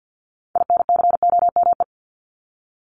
Morse code, heard as a keyed tone, about 50 wpm, 700 Hz. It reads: INCOME
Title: Top 751-1000 Words 50wpm